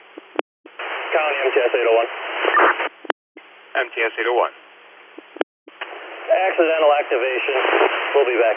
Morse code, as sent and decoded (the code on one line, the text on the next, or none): none